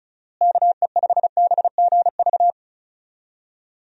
Morse code, heard as a keyed tone, about 34 wpm, 700 Hz. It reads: KE5BGV
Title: USA Call Signs - 6 Characters - Repeated 2x - 35wpm